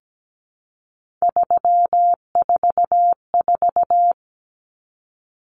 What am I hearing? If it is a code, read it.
Morse code, heard as a keyed tone, about 17 wpm, 700 Hz. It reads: 344
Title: Numbers - 3-Digit 17wpm